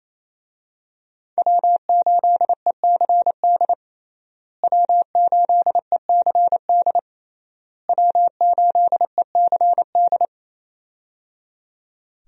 Morse code, heard as a keyed tone, about 28 wpm, 700 Hz. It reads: W8ECD W8ECD W8ECD
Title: USA Call Signs - 5 Characters - Repeated 3x - 28wpm